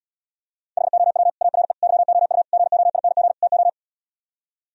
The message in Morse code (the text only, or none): HOW ARE YOU GOING TO